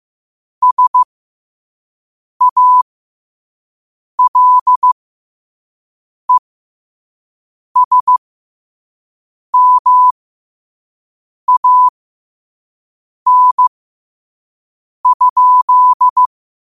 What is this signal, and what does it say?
Morse code, heard as a keyed tone, about 15 wpm, spaced as for 5 wpm, 1000 Hz. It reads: SALESMAN?
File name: qotdfiles / 5wpm